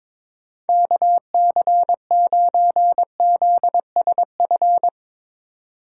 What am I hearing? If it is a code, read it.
Morse code, heard as a keyed tone, about 22 wpm, 700 Hz. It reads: KC9ZSF
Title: USA Call Signs - 6 Characters - 3x Character Spacing - 22wpm